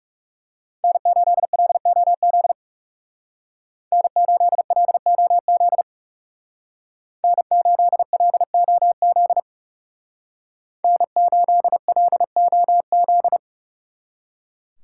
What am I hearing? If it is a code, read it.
Morse code, heard as a keyed tone, about 36 wpm, 700 Hz. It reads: N8LOZ N8LOZ N8LOZ N8LOZ
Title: Speed Racing - USA 1x3 Call Signs 30wpm